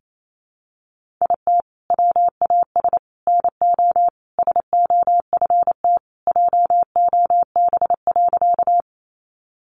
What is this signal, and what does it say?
Morse code, heard as a keyed tone, about 28 wpm, 700 Hz. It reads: IT WAS NO SOFT JOB.
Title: Sentences from Top 1500 Words 28wpm